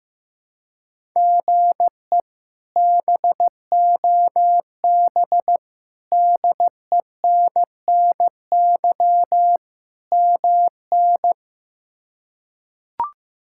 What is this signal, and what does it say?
Morse code, heard as a keyed tone, about 15 wpm, 700 Hz. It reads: GE BOB DENNY MN
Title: K1USN SST QSO Practice - 15wpm